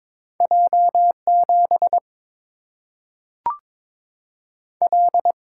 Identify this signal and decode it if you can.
Morse code, heard as a keyed tone, about 22 wpm, 700 Hz. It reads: J 7 L
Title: Single-Letter Single-Number 22wpm